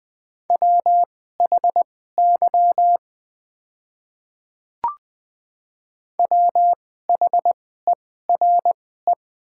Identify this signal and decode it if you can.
Morse code, heard as a keyed tone, about 20 wpm, spaced as for 14 wpm, 700 Hz. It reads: WHY WHERE
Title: 2x Character Spacing - Top 200 Words - 20wpm